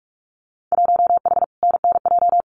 Morse code, heard as a keyed tone, about 46 wpm, 700 Hz. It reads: 1H NNJ
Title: Winter Field Day Exchanges 45wpm